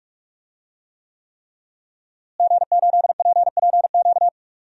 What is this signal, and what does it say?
Morse code, heard as a keyed tone, about 43 wpm, 700 Hz. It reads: G8PPQ
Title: DXCC-100 Call Signs - 5 Characters - Repeated 3x - 45wpm